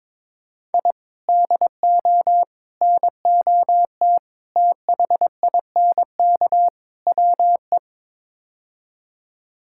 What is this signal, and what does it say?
Morse code, heard as a keyed tone, about 22 wpm, 700 Hz. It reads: I DO NOT THINK WE
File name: practice / 22wpm / Build A Phrase - 5-Word Phrases - 22wpm